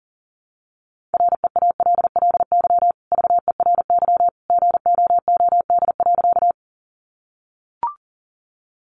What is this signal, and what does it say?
Morse code, heard as a keyed tone, about 40 wpm, 700 Hz. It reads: REALLY VERY GOOD.
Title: Sentences from Top 600 Words 40wpm